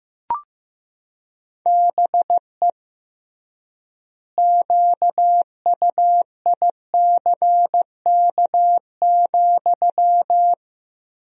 Morse code, heard as a keyed tone, about 15 wpm, 700 Hz. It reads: BE QUICK,
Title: 3x Word-Spacing - Sentences from Top 500 Words - 15wpm